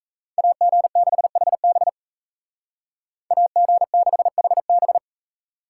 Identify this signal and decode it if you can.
Morse code, heard as a keyed tone, about 36 wpm, 700 Hz. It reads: AG6HB AG6HB